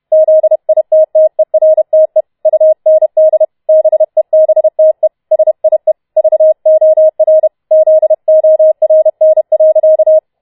Morse code, heard as a keyed tone, about 31 wpm, 630 Hz. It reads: ZITTERTE UND BEBTE SIE VOR ZORN.